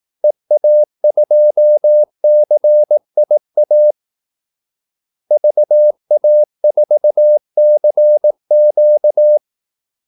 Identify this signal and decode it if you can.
Morse code, heard as a keyed tone, about 18 wpm, 600 Hz. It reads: EA2CIA VA4CQ
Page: Kaldesignaler 18 wpm